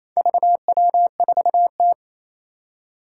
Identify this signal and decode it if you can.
Morse code, heard as a keyed tone, about 28 wpm, 700 Hz. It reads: VW4T
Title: Four Challenging Letter-Number Combinations 28wpm